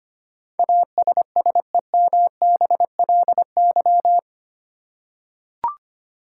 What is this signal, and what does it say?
Morse code, heard as a keyed tone, about 25 wpm, 700 Hz. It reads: ASSEMBLY
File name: 8 Letter Words - First 3 Letters Given - 25wpm